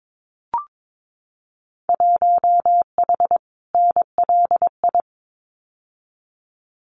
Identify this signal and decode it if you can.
Morse code, heard as a keyed tone, about 22 wpm, 700 Hz. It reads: 1H NLI
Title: Winter Field Day Exchanges 22wpm